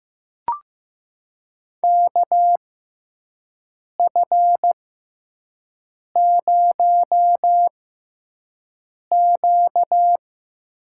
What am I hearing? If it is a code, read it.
Morse code, heard as a keyed tone, about 15 wpm, spaced as for 5 wpm, 700 Hz. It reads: KF0Q
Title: USA Call Signs - 6x Character Spacing - 15wpm